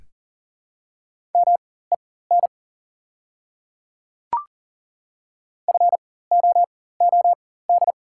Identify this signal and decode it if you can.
Morse code, heard as a keyed tone, about 40 wpm, spaced as for 19 wpm, 700 Hz. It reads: MEN FOOD
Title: 4x Character Spacing - Top 300 Words - 40wpm